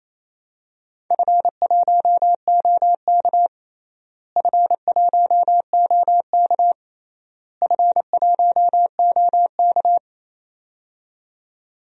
Morse code, heard as a keyed tone, about 28 wpm, 700 Hz. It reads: F1OK F1OK F1OK
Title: DXCC-100 Call Signs - 4 Characters - Repeated 3x - 28wpm